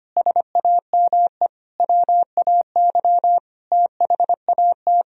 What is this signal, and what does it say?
Morse code, heard as a keyed tone, about 31 wpm, 700 Hz. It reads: SAME WAY THAT